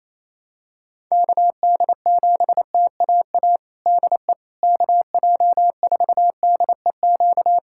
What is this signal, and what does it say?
Morse code, heard as a keyed tone, about 28 wpm, 700 Hz. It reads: KD7TAA DE KJ4DEQ
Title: USA Callsign DE Callsign 28wpm